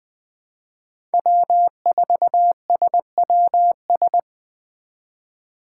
Morse code, heard as a keyed tone, about 20 wpm, 700 Hz. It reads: W4SWS
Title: USA Call Signs - 5 Characters - 3x Character Spacing - 20wpm